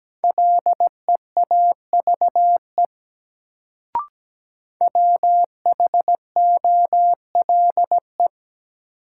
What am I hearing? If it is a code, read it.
Morse code, heard as a keyed tone, about 17 wpm, 700 Hz. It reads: LEAVE WHOLE
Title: Sentences from Top 400 Words - New Word Drill - 17wpm